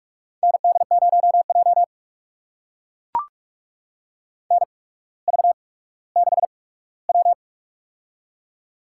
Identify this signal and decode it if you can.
Morse code, heard as a keyed tone, about 44 wpm, spaced as for 19 wpm, 700 Hz. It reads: ND0J NV6W
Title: USA Call Signs - 4 Characters - 8x Character Spacing - 45wpm